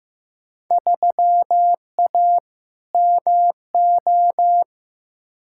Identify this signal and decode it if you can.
Morse code, heard as a keyed tone, about 15 wpm, 700 Hz. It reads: 3A MO
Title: Field Day Exchanges Encore - 15wpm